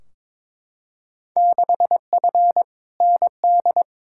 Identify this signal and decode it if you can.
Morse code, heard as a keyed tone, about 21 wpm, 700 Hz. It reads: 6F ND
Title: Field Day Exchanges Encore - 22wpm